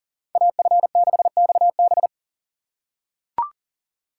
Morse code, heard as a keyed tone, about 40 wpm, 700 Hz. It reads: AF6XB
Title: USA Call Signs - 5 Characters - 6x Character Spacing - 40wpm